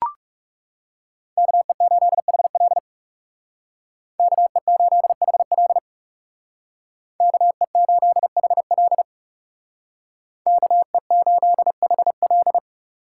Morse code, heard as a keyed tone, about 37 wpm, 700 Hz. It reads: KE8HL KE8HL KE8HL KE8HL
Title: Speed Racing - USA Call Signs 30wpm